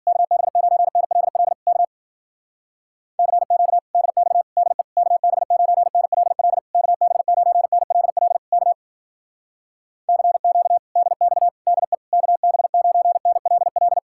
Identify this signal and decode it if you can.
Morse code, heard as a keyed tone, about 48 wpm, 700 Hz. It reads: KB9NLL K CQ DX DE KB9NLL KB9NLL K CQ DX DE KB9NLL
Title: Speed-Racing Common QSO Phrases - 40wpm